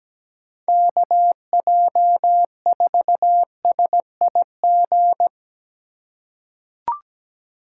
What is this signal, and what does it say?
Morse code, heard as a keyed tone, about 17 wpm, 700 Hz. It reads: KJ4SIG
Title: USA Call Signs - 6 Characters - 4x Character Spacing - 17wpm